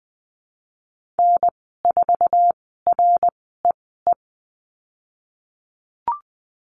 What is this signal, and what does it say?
Morse code, heard as a keyed tone, about 20 wpm, spaced as for 15 wpm, 700 Hz. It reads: N4REE
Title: 055 Lesson 10g - Letter r - Call Signs 20wpm